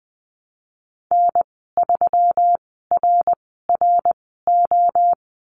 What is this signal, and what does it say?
Morse code, heard as a keyed tone, about 20 wpm, spaced as for 15 wpm, 700 Hz. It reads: N3RRO
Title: 161 Lesson 26g - Letter g - Call Signs 20wpm